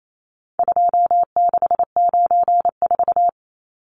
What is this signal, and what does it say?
Morse code, heard as a keyed tone, about 28 wpm, 700 Hz. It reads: 2694
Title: Numbers - 4-Digit 28wpm